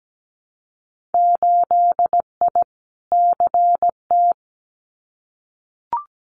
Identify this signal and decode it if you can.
Morse code, heard as a keyed tone, about 17 wpm, 700 Hz. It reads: 8I CT
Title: Winter Field Day Exchanges Encore 17wpm